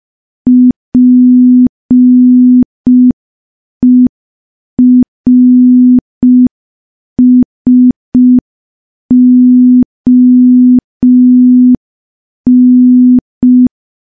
Morse code, heard as a keyed tone, about 5 wpm, 250 Hz. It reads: PERSON
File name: morse / morse-training-05WPM